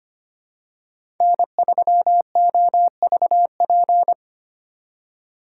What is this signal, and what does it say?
Morse code, heard as a keyed tone, about 25 wpm, 700 Hz. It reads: N3OVP
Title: USA Call Signs - 5 Characters - 3x Character Spacing - 25wpm